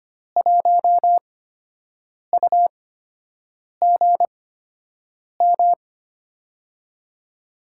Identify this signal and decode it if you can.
Morse code, heard as a keyed tone, about 25 wpm, spaced as for 7 wpm, 700 Hz. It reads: 1UGM
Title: USA Call Signs - 5 Characters - 8x Character Spacing - 25wpm